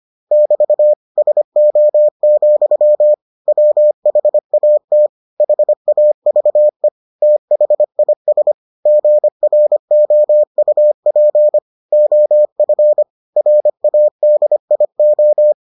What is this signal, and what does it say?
Morse code, heard as a keyed tone, about 25 wpm, 600 Hz. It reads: <BT> SO, WHAT HAVE THIS GROUP OF RADIO